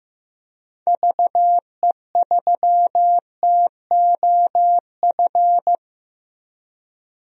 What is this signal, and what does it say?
Morse code, heard as a keyed tone, about 15 wpm, 700 Hz. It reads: VE3TOF
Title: DXCC-100 Call Signs - 6 Characters - Repeated 3x - 15wpm